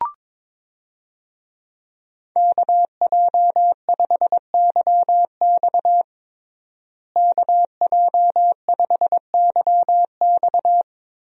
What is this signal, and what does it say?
Morse code, heard as a keyed tone, about 22 wpm, 700 Hz. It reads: KJ5YX KJ5YX
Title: USA Call Signs - 5 Characters - Repeated 2x - 22wpm